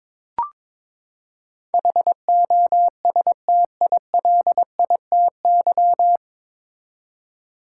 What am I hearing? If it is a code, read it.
Morse code, heard as a keyed tone, about 22 wpm, 700 Hz. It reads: HOSTILITY
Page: Morse Code Ninja practice 22wpm